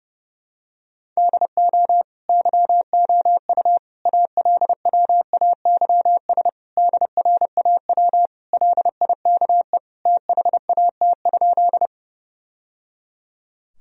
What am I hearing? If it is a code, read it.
Morse code, heard as a keyed tone, about 30 wpm, 700 Hz. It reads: DO YOU ALWAYS DRAW LIKE THAT?
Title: Sentences from Top 1000 Words - Review Part 1 - 30wpm